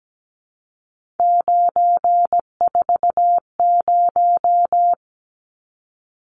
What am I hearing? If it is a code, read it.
Morse code, heard as a keyed tone, about 17 wpm, 700 Hz. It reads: 940
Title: Numbers - 3-Digit 17wpm